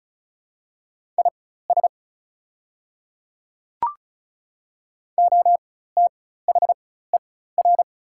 Morse code, heard as a keyed tone, about 36 wpm, spaced as for 16 wpm, 690 Hz. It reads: IS OTHER
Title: 4x Character Spacing - Top 100 Words - 35wpm